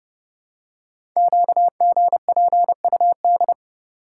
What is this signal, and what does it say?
Morse code, heard as a keyed tone, about 30 wpm, 700 Hz. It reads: QGPUD
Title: Code Groups - 5 Letters - 30wpm